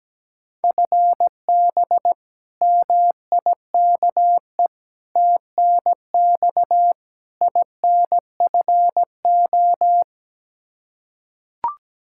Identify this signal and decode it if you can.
Morse code, heard as a keyed tone, about 17 wpm, 700 Hz. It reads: FB MIKE TNX INFO
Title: Practice QSO - Exchange 3 - More Info - 17wpm